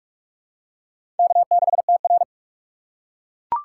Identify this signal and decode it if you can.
Morse code, heard as a keyed tone, about 41 wpm, 690 Hz. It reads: K6TR
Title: USA Call Signs - 4 Characters - 2x Character Spacing - 45wpm